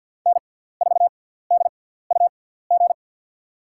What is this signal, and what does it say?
Morse code, heard as a keyed tone, about 52 wpm, spaced as for 17 wpm, 700 Hz. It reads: N4DUG
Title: USA Call Signs - 5 Characters - 6x Character Spacing - 50wpm